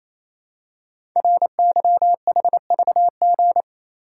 Morse code, heard as a keyed tone, about 27 wpm, 700 Hz. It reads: RYHVG
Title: Code Groups - 5 Letters - 28wpm